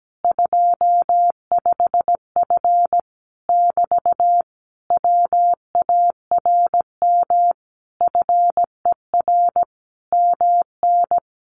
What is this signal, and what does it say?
Morse code, heard as a keyed tone, about 17 wpm, 700 Hz. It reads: 25F <BT> WARM FER MN
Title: Common QSO Phrases Encore 17wpm